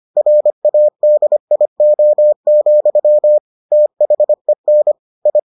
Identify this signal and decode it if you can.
Morse code, heard as a keyed tone, about 25 wpm, 600 Hz. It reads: RADIO, THEN I